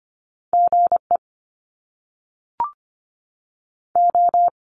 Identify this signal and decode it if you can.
Morse code, heard as a keyed tone, about 25 wpm, 700 Hz. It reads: G E O
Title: Top Two Letter Combinations in Words 25wpm